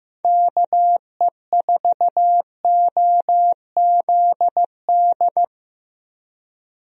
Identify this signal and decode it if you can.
Morse code, heard as a keyed tone, about 15 wpm, 700 Hz. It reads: KE4OZD
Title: USA Call Signs - 6 Characters - Repeated 3x - 15wpm